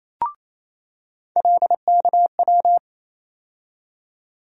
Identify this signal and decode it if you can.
Morse code, heard as a keyed tone, about 28 wpm, 700 Hz. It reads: LKW
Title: USA Call Sign Postfixes - 3 Letters 28wpm